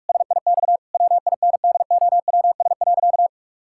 Morse code, heard as a keyed tone, about 46 wpm, 700 Hz. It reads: SIX WINDOWS.
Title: Pangram Challenge v2 - 45wpm